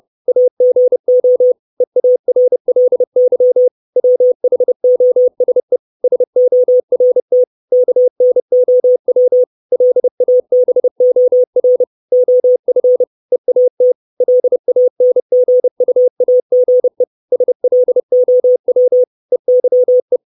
Morse code, heard as a keyed tone, about 30 wpm, 500 Hz. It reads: AGO EARLY WHOSE SORT KNOW LABOR OF EAT LANGUAGE SLOW EYE
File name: 632 Words 30WPM